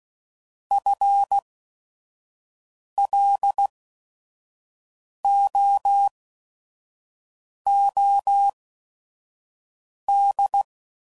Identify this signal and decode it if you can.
Morse code, heard as a keyed tone, about 16 wpm, spaced as for 5 wpm, 800 Hz. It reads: FLOOD
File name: Prefixes / By First Letter / Fs